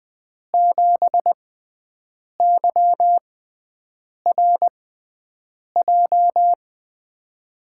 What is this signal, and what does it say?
Morse code, heard as a keyed tone, about 21 wpm, spaced as for 7 wpm, 700 Hz. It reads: 7YRJ
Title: USA Call Signs - 5 Characters - 6x Character Spacing - 20wpm